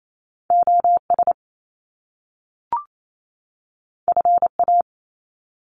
Morse code, heard as a keyed tone, about 28 wpm, 700 Hz. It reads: OS FA
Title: Top Two Letter Combinations in Words 28wpm